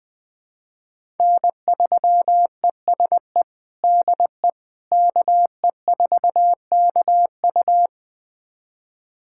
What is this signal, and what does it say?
Morse code, heard as a keyed tone, about 21 wpm, 700 Hz. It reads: N3ESE DE KE4KU
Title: USA Callsign DE Callsign - 5 Characters - 20wpm